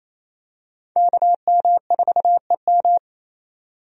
Morse code, heard as a keyed tone, about 28 wpm, 700 Hz. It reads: KM4EM
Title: USA Call Signs - 5 Characters - 28wpm